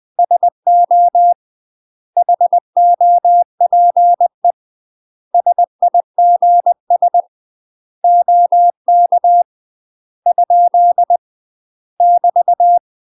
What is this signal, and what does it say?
Morse code, heard as a keyed tone, about 20 wpm, 700 Hz. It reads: SO HOPE SIGS OK ? <BT>
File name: UK MASTER QSOs TRAINING / 20WPM / UK MASTER QSOs 0026